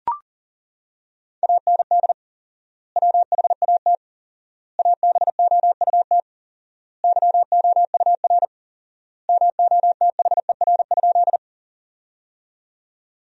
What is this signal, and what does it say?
Morse code, heard as a keyed tone, about 40 wpm, 700 Hz. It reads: AND WHAT ABOUT YOUR MOTHER?
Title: 4x Word-Spacing - Sentences from Top 200 Words - 40wpm